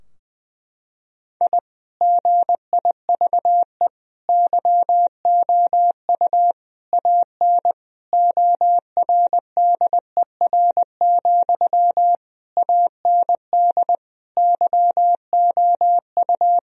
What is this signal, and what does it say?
Morse code, heard as a keyed tone, about 20 wpm, 700 Hz. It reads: I GIVE YOU AN ORDER, AND YOU
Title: Sentences from Top 1000 Words - Review Part 2 - 20wpm